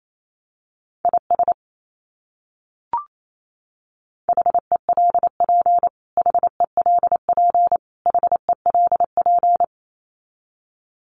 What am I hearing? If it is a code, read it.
Morse code, heard as a keyed tone, about 28 wpm, 700 Hz. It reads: IS HELP HELP HELP
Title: Top 200 Words Repeated 28wpm